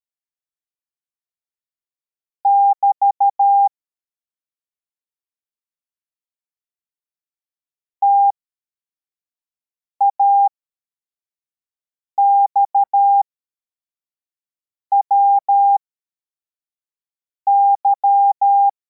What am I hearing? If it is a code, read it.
Morse code, heard as a keyed tone, about 13 wpm, spaced as for 4 wpm, 800 Hz. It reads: <BT> TAXWY